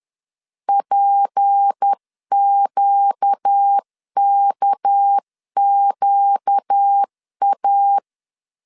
Morse code, heard as a keyed tone, about 10 wpm, 800 Hz. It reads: PQKQA
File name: CW / DET / 1 6 1500C S C 10wpm 4252